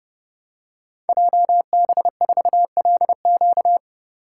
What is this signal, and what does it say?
Morse code, heard as a keyed tone, about 30 wpm, 700 Hz. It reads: JB4LQ
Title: Code Groups - 5 Letter:Numbers - 30wpm